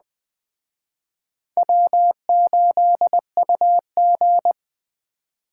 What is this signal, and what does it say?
Morse code, heard as a keyed tone, about 24 wpm, 700 Hz. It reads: W8UG
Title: Speed Racing - USA Call Signs 20wpm